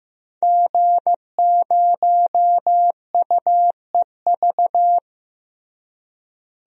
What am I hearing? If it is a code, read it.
Morse code, heard as a keyed tone, about 16 wpm, 700 Hz. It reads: G0UEV
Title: DXCC-100 Call Signs - 5 Characters - Repeated 3x - 15wpm